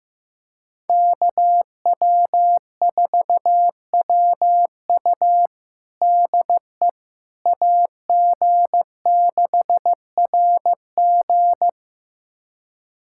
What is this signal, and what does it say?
Morse code, heard as a keyed tone, about 15 wpm, 700 Hz. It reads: KW4WU DE AG6RG
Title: USA Callsign DE Callsign - 5 Characters - 15wpm